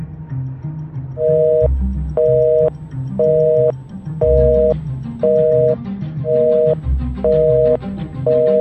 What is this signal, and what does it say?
Busy tone (North America), 480 + 620 Hz.